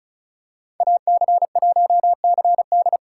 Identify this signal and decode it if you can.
Morse code, heard as a keyed tone, about 34 wpm, 700 Hz. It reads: AC1CD
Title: USA Call Signs - 2x Character Spacing - 35wpm